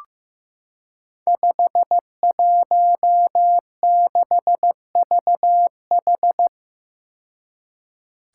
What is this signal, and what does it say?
Morse code, heard as a keyed tone, about 15 wpm, 700 Hz. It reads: 516VH
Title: Five Challenging Letter-Number Combinations 15wpm